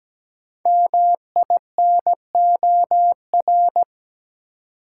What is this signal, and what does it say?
Morse code, heard as a keyed tone, about 17 wpm, 700 Hz. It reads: MINOR
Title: Five Letter Words - 17wpm